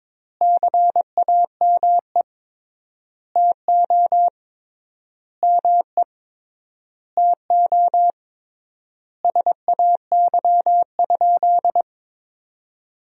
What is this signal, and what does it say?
Morse code, heard as a keyed tone, about 22 wpm, 700 Hz. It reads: CAME TO ME TO SAY?
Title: 3x Word-Spacing - Sentences from Top 200 Words - 22wpm